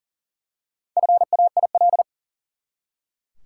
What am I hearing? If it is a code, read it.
Morse code, heard as a keyed tone, about 40 wpm, 700 Hz. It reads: FAIL
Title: Four Letter Words - 40wpm